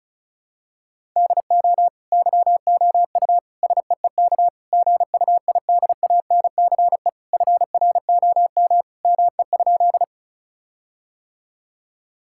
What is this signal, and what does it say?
Morse code, heard as a keyed tone, about 36 wpm, 700 Hz. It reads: DO YOU SEEK GUIDANCE FROM ME?
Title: Sentences from Top 1600 Words 35wpm